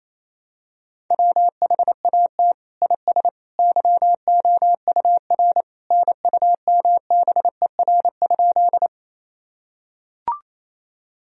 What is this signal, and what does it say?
Morse code, heard as a keyed tone, about 28 wpm, 700 Hz. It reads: WHAT IS YOUR NUMBER?
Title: Sentences from Top 1000 Words - Review Part 1 - 28wpm